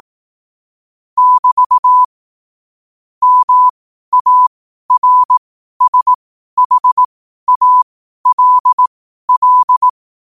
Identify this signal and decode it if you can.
Morse code, heard as a keyed tone, about 18 wpm, spaced as for 12 wpm, 1000 Hz. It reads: <BT> MARSHALL
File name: qotdfiles / 12wpm